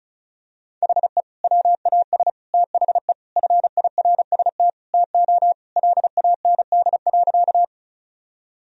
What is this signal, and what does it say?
Morse code, heard as a keyed tone, about 36 wpm, 700 Hz. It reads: HE WAS THE FIRST TO LAND.
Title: Sentences from Top 1000 Words - Review Part 1 - 35wpm